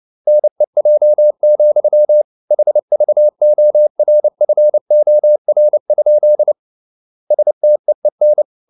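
Morse code, heard as a keyed tone, about 29 wpm, 600 Hz. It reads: NEJ, HVORFOR? STEEN
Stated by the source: Fast (QRQ) Morse at 29 wpm, from CW med Gnister.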